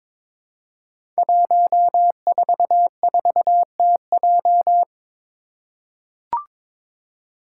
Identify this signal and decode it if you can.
Morse code, heard as a keyed tone, about 22 wpm, 700 Hz. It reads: 144TJ
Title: Five Challenging Letter-Number Combinations 22wpm